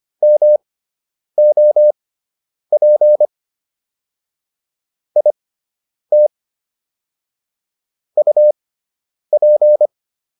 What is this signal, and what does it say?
Morse code, heard as a keyed tone, about 25 wpm, spaced as for 9 wpm, 600 Hz. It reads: MOP IT UP